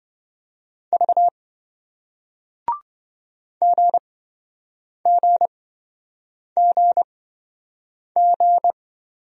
Morse code, heard as a keyed tone, about 27 wpm, spaced as for 7 wpm, 700 Hz. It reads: V GGGG